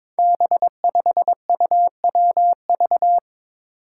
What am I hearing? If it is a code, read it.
Morse code, heard as a keyed tone, about 22 wpm, 700 Hz. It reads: B5UWV
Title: Five Challenging Letter-Number Combinations 22wpm